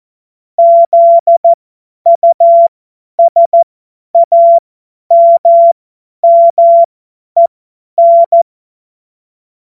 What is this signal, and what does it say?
Morse code, heard as a keyed tone, about 14 wpm, spaced as for 10 wpm, 680 Hz. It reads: ZUSAMMEN
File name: Morsekurs / Witze / Witz 25. Juli 2023 50 F70